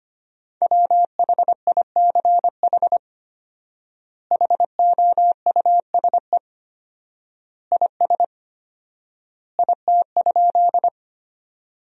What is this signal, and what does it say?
Morse code, heard as a keyed tone, about 25 wpm, 700 Hz. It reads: WHICH HOUSE IS IT?
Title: 4x Word-Spacing - Sentences from Top 200 Words - 25wpm